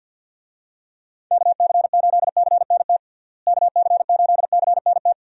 Morse code, heard as a keyed tone, about 52 wpm, 700 Hz. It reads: KC8CNT KC8CNT
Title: USA Call Signs - 6 Characters - Repeated 2x - 50wpm